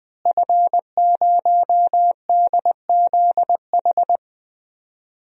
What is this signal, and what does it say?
Morse code, heard as a keyed tone, about 20 wpm, 700 Hz. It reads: F0DZH
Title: DXCC-100 Call Signs - 5 Characters - Repeated 2x - 20wpm